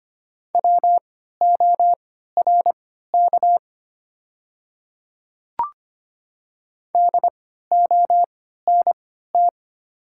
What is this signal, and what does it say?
Morse code, heard as a keyed tone, about 25 wpm, spaced as for 14 wpm, 700 Hz. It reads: WORK DONT
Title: 3x Character Spacing - Top 300 Words - 25wpm